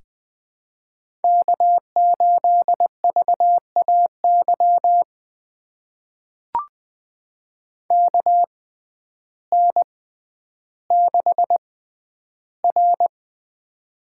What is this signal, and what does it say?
Morse code, heard as a keyed tone, about 20 wpm, spaced as for 11 wpm, 700 Hz. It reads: K8VAY KN6R
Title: USA Call Signs - 6x Character Spacing - 20wpm